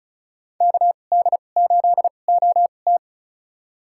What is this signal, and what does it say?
Morse code, heard as a keyed tone, about 36 wpm, spaced as for 25 wpm, 700 Hz. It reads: KD8OT
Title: USA Call Signs - 5 Characters - 2x Character Spacing - 35wpm